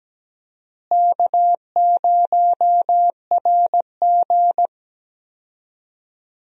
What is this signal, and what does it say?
Morse code, heard as a keyed tone, about 17 wpm, 700 Hz. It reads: K0RG
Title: USA Call Signs - 4 Characters - Repeated 2x - 17wpm